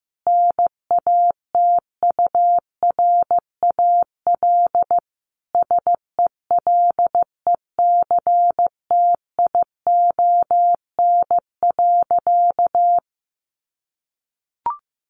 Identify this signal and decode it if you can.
Morse code, heard as a keyed tone, about 15 wpm, 700 Hz. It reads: NATURAL SELECTION.
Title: Sentences from Top 1400 Words 15wpm